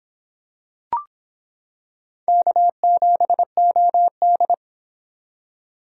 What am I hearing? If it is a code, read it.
Morse code, heard as a keyed tone, about 26 wpm, 700 Hz. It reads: K7OD